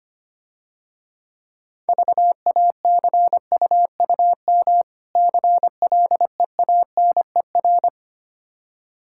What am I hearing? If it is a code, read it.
Morse code, heard as a keyed tone, about 25 wpm, 700 Hz. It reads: VACUUM CLEANER
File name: Easy Words - Speed-Racing with Hints - 25wpm